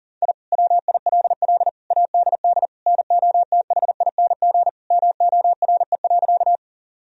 Morse code, heard as a keyed tone, about 37 wpm, 700 Hz. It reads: I WILL ADD NOTHING MORE.
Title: Sentences from Top 1000 Words - Review Part 1 - 40wpm